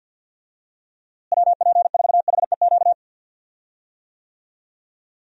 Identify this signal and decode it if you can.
Morse code, heard as a keyed tone, about 51 wpm, 700 Hz. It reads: WP4HEQ
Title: USA Call Signs - 6 Characters - Repeated 2x - 50wpm